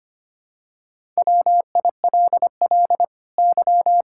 Morse code, heard as a keyed tone, about 25 wpm, 700 Hz. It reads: WILL Y
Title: Sentences from Top 1000 Words - Review Part 6 - 25wpm